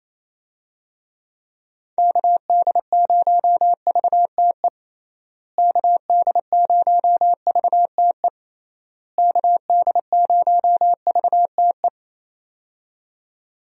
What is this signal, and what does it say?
Morse code, heard as a keyed tone, about 28 wpm, 700 Hz. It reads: KD0VTE KD0VTE KD0VTE
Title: USA Call Signs - 6 Characters - Repeated 3x - 28wpm